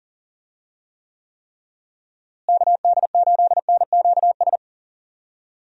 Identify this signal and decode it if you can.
Morse code, heard as a keyed tone, about 40 wpm, 700 Hz. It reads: KD8NQS
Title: USA Call Signs - 6 Characters - Repeated 3x - 40wpm